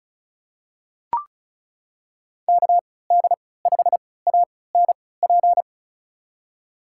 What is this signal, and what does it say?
Morse code, heard as a keyed tone, about 36 wpm, spaced as for 20 wpm, 700 Hz. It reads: KD5ANP
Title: USA Call Signs - 3x Character Spacing - 35wpm